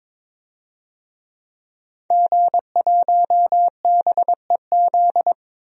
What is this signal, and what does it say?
Morse code, heard as a keyed tone, about 21 wpm, 700 Hz. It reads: G1BEZ
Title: DXCC-100 Call Signs - 5 Characters - Repeated 2x - 22wpm